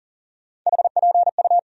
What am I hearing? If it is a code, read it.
Morse code, heard as a keyed tone, about 40 wpm, 700 Hz. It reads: HPU
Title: USA Call Sign Postfixes - 3 Letters 40wpm